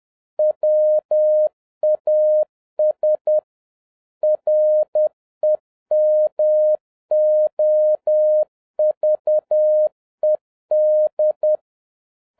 Morse code, heard as a keyed tone, about 10 wpm, 610 Hz. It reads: WAS REMOVED